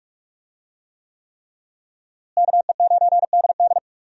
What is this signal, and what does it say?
Morse code, heard as a keyed tone, about 46 wpm, 700 Hz. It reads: KE9DD
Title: USA Call Signs - 5 Characters - Repeated 3x - 45wpm